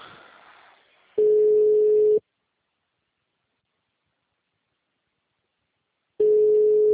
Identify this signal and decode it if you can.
Ringback tone (Europe, 425 Hz).
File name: out- -2006- - - .15577